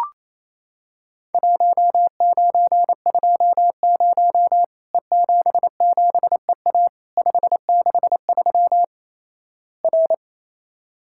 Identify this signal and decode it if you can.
Morse code, heard as a keyed tone, about 28 wpm, 700 Hz. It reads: 1920 E77EA 563 R
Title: Worked All Europe DX Contest - QTC Exchange - 28wpm